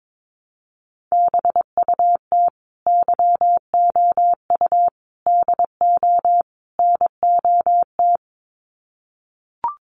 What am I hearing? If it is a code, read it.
Morse code, heard as a keyed tone, about 22 wpm, 700 Hz. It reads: BUT YOU DO NOT
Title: Sets of 4 Words 22wpm